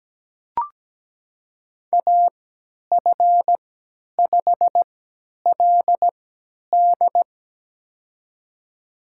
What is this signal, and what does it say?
Morse code, heard as a keyed tone, about 17 wpm, spaced as for 10 wpm, 700 Hz. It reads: AF5LD
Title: USA Call Signs - 3x Character Spacing - 17wpm